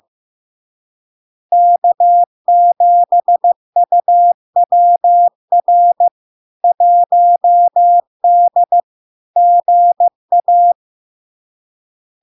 Morse code, heard as a keyed tone, about 15 wpm, 700 Hz. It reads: K7UWR 1D GA
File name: LARK-FD-2022-Log-in-CW-15wpm-700-Hz